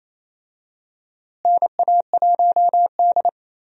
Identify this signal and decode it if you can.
Morse code, heard as a keyed tone, about 30 wpm, 700 Hz. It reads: NA1D
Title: USA Call Signs - 4 Characters - Repeated 3x - 28wpm